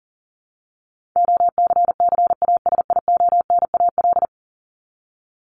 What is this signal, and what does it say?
Morse code, heard as a keyed tone, about 40 wpm, 700 Hz. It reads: OCCASIONAL
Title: LY Suffix - Top 100 Words with Base Words 40wpm